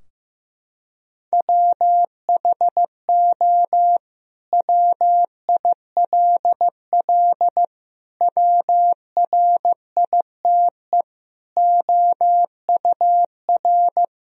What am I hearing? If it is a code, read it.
Morse code, heard as a keyed tone, about 15 wpm, 700 Hz. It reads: WHO WILL WRITE OUR
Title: Sentences from Top 1000 Words - Review Part 3 - 15wpm